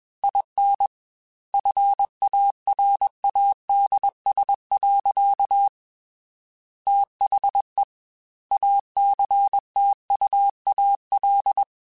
Morse code, heard as a keyed tone, about 21 wpm, 800 Hz. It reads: IN FARADS. THE ACTUAL
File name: Technical Subjects / Technical Subjects-mp3s / XC @20